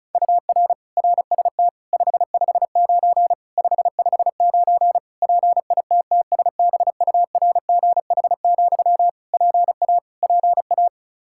Morse code, heard as a keyed tone, about 34 wpm, 700 Hz. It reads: UR RST 559 559 PITTSBURGH, PA PA
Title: Practice QSO - Exchange 1 - Beginning - 35wpm